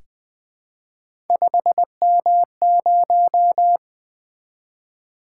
Morse code, heard as a keyed tone, about 20 wpm, 700 Hz. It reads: 5M0
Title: Three Letter-Number Combinations - 20wpm